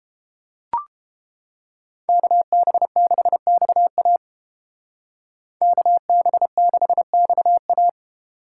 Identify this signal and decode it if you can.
Morse code, heard as a keyed tone, about 30 wpm, 700 Hz. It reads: KB6XA KB6XA